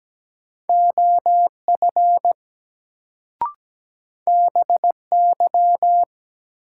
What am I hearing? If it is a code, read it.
Morse code, heard as a keyed tone, about 17 wpm, 700 Hz. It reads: OF BY
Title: Sentences from Top 100 Words - New Word Drill - 17wpm